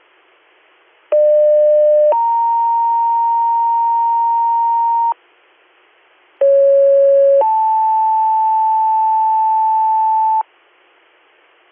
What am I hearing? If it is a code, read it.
Two-tone page, heard as a one-second tone, then a three-second tone, about 600 then 930 Hz, about 550 then 880 Hz. It is heard from 1.1 to 10.4 s.
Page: allegany-ny